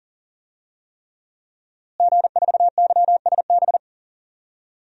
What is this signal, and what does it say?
Morse code, heard as a keyed tone, about 40 wpm, 700 Hz. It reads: G4YSB
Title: DXCC-100 Call Signs - 5 Characters - Repeated 3x - 40wpm